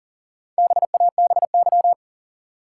Morse code, heard as a keyed tone, about 40 wpm, 700 Hz. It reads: BABY
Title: Four Letter Words - 40wpm